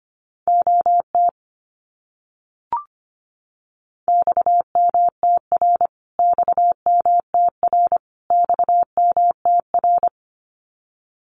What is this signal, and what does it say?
Morse code, heard as a keyed tone, about 25 wpm, 700 Hz. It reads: OT XMTR XMTR XMTR
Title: Abbreviations Repeated 3x - 25wpm